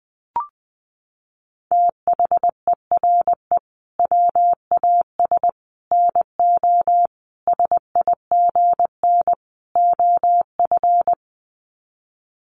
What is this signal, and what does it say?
Morse code, heard as a keyed tone, about 20 wpm, 700 Hz. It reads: THERE WAS NO SIGN OF
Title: Sets of 5 Words Encore2 20wpm